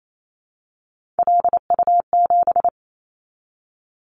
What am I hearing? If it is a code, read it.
Morse code, heard as a keyed tone, about 28 wpm, 700 Hz. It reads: LU7
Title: Callsign Prefixes for Top 100 DXCC - 28wpm